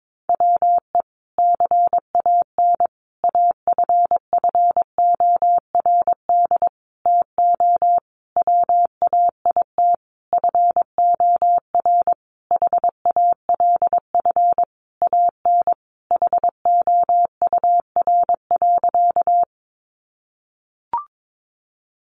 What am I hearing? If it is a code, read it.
Morse code, heard as a keyed tone, about 22 wpm, 700 Hz. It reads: WE CAN AFFORD TO WAIT FOR HALF AN HOUR.
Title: Sentences from Top 1100 Words 22wpm